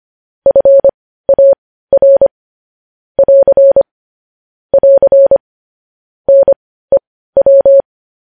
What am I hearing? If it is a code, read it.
Morse code, heard as a keyed tone, about 25 wpm, spaced as for 15 wpm, 550 Hz. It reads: FAR <AR> <AR> NEW
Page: news headlines in morse code at 15 wpm